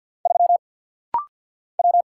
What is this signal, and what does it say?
Morse code, heard as a keyed tone, about 48 wpm, 700 Hz. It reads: 3 W
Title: Single Letter-Number - Rapid-Fire 50wpm